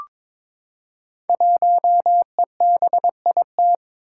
Morse code, heard as a keyed tone, about 22 wpm, 700 Hz. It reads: 1EBIT
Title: Five Challenging Letter-Number Combinations 22wpm